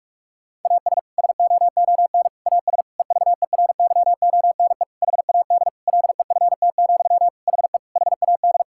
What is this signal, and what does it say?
Morse code, heard as a keyed tone, about 41 wpm, 700 Hz. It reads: AS SOON AS EVERYONE HAD LEFT, HE HAD
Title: Sentences from Top 1000 Words - Review Part 5 - 45wpm